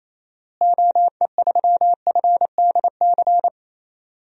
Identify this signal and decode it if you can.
Morse code, heard as a keyed tone, about 28 wpm, 700 Hz. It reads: OE3FDC
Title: DXCC-100 Call Signs - 6 Characters - Repeated 2x - 28wpm